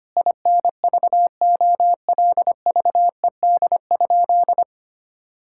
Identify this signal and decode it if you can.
Morse code, heard as a keyed tone, about 25 wpm, 700 Hz. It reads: INVOLVED?
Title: Sentences from Top 1000 Words - Review Part 5 - 25wpm